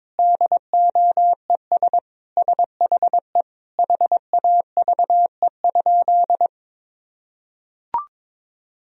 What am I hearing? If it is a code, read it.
Morse code, heard as a keyed tone, about 22 wpm, 700 Hz. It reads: DOES SHE HAVE?
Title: Sentences from Top 1000 Words - Review Part 1 - 22wpm